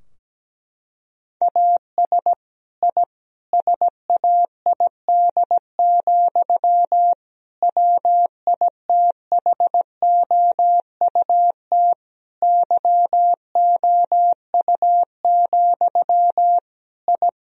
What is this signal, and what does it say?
Morse code, heard as a keyed tone, about 17 wpm, 700 Hz. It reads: AS I SAID, WITHOUT YOU, I
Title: Sentences from Top 1000 Words - Review Part 3 - 17wpm